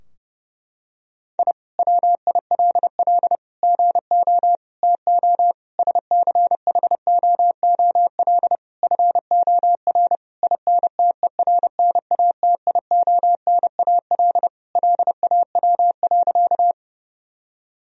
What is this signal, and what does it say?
Morse code, heard as a keyed tone, about 30 wpm, 700 Hz. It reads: I WILL GO TO SCHOOL FOR INTERNATIONAL LAW.
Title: Sentences from Top 1000 Words - Review Part 4 - 30wpm